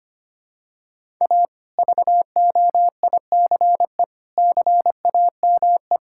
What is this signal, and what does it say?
Morse code, heard as a keyed tone, about 25 wpm, 700 Hz. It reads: A VOICE CAME
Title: Sentences from Top 1000 Words - Review Part 5 - 25wpm